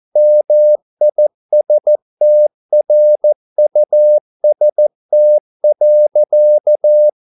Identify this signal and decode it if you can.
Morse code, heard as a keyed tone, about 14 wpm, 600 Hz. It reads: MISTRUST.